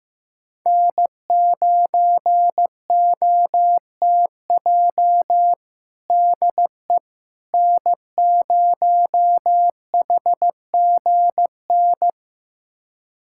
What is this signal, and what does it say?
Morse code, heard as a keyed tone, about 15 wpm, 700 Hz. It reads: N9OTJ DE N0HGN
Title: USA Callsign DE Callsign - 5 Characters - 15wpm